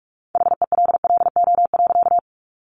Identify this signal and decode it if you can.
Morse code, heard as a keyed tone, about 44 wpm, 700 Hz. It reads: HELLO.
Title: Sentences from Top 1100 Words 45wpm